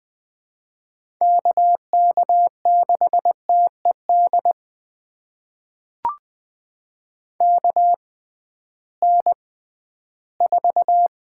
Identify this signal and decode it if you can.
Morse code, heard as a keyed tone, about 20 wpm, spaced as for 13 wpm, 700 Hz. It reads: KK6TED KN4
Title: USA Call Signs - 6 Characters - 6x Character Spacing - 20wpm